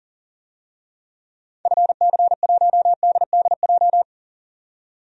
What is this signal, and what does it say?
Morse code, heard as a keyed tone, about 38 wpm, 700 Hz. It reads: FC1DDJ
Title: DXCC-100 Call Signs - 6 Characters - 40wpm